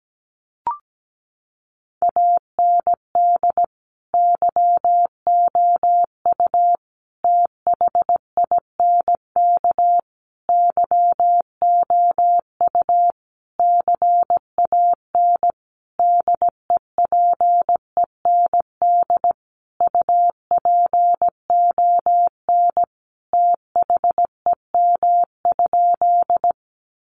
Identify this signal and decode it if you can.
Morse code, heard as a keyed tone, about 17 wpm, 700 Hz. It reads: AND YOU THINK YOU CAN DEPEND UPON THEM?
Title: Sentences from Top 1300 Words 17wpm